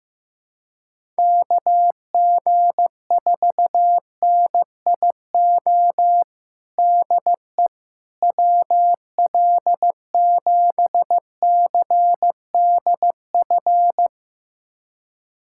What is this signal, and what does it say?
Morse code, heard as a keyed tone, about 16 wpm, 700 Hz. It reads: KG4NIO DE WL7CDF
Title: USA Callsign DE Callsign - 6 Characters - 15wpm